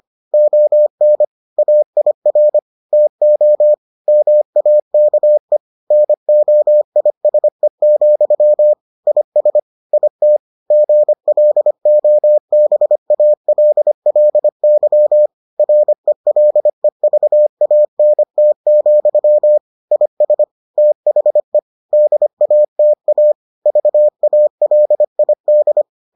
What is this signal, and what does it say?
Morse code, heard as a keyed tone, about 25 wpm, 600 Hz. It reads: ON AIR TO MAKE NOISE, IS IT GLOBALLY RELEVANT, IS THE DATA VALID